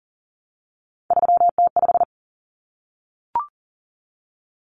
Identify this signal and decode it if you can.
Morse code, heard as a keyed tone, about 40 wpm, 700 Hz. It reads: 3T5
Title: Common Cut Numbers - 3-Digit 40wpm